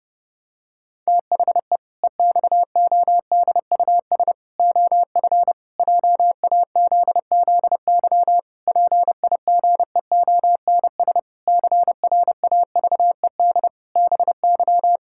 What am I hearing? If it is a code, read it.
Morse code, heard as a keyed tone, about 30 wpm, 700 Hz. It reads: THE EXODUS OF JAZZY PIGEONS CRAVED BY
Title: Pangram Challenge v2 - 30wpm